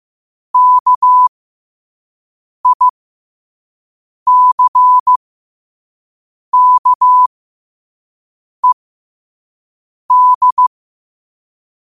Morse code, heard as a keyed tone, about 15 wpm, spaced as for 5 wpm, 1000 Hz. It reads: KICKED
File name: qotdfiles / 5wpm